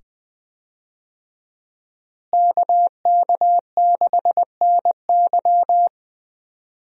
Morse code, heard as a keyed tone, about 20 wpm, 700 Hz. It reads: KK6NY
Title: USA Call Signs - 5 Characters - Repeated 2x - 20wpm